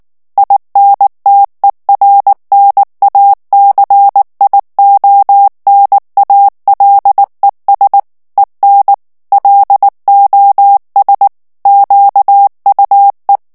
Morse code, heard as a keyed tone, about 19 wpm, 800 Hz. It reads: INTERNACIONALES EN LOS QUE
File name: morse mp3 / ESPAGNOL TEXTE EN CLAIR 2 VITESSE 18 MOTS MINUTES